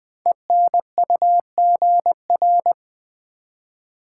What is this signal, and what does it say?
Morse code, heard as a keyed tone, about 20 wpm, 700 Hz. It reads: ENUGR
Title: Code Groups - 5 Letters - 20wpm